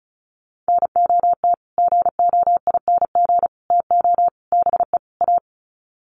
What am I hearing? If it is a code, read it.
Morse code, heard as a keyed tone, about 34 wpm, 700 Hz. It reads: NOT GOING TO BE A
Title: Sets of 5 Words Encore 35wpm